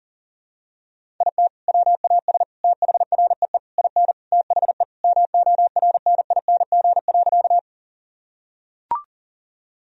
Morse code, heard as a keyed tone, about 40 wpm, 700 Hz. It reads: IT WAS THREE IN THE MORNING.
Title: Sentences from Top 1000 Words - Review Part 1 - 40wpm